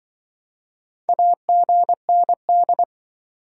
Morse code, heard as a keyed tone, about 28 wpm, 700 Hz. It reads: AGND